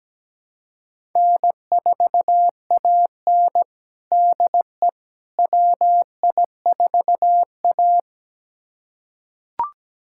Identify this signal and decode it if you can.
Morse code, heard as a keyed tone, about 17 wpm, 700 Hz. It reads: N4AN DE WI4A
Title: USA Callsign DE Callsign - 4 Characters - 17wpm